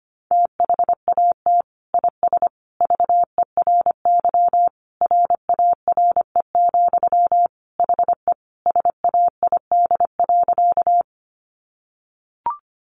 Morse code, heard as a keyed tone, about 25 wpm, 700 Hz. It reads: THAT IS VERY RARE, HE SAID.
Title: Sentences from Top 1100 Words 25wpm